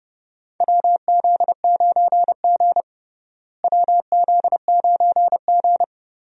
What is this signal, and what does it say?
Morse code, heard as a keyed tone, about 30 wpm, 700 Hz. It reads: WZ9G WZ9G
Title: USA Call Signs - 4 Characters - Repeated 3x - 30wpm